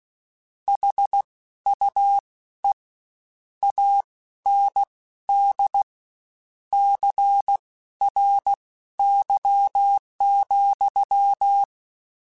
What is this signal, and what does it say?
Morse code, heard as a keyed tone, about 16 wpm, spaced as for 12 wpm, 800 Hz. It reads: HUE AND CRY,